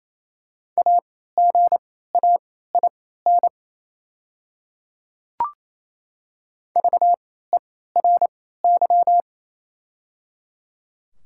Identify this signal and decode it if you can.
Morse code, heard as a keyed tone, about 28 wpm, spaced as for 16 wpm, 690 Hz. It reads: AGAIN VERY
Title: 3x Character Spacing - Top 300 Words - 28wpm